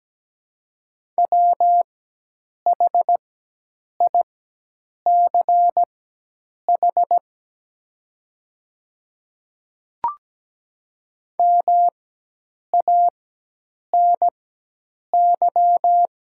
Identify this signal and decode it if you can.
Morse code, heard as a keyed tone, about 17 wpm, spaced as for 8 wpm, 700 Hz. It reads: WHICH MANY
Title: 4x Character Spacing - Top 100 Words - 17wpm